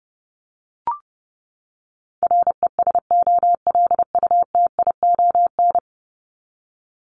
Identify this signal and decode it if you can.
Morse code, heard as a keyed tone, about 30 wpm, 700 Hz. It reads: RESOLUTION
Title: Top 1501-1750 Words 30wpm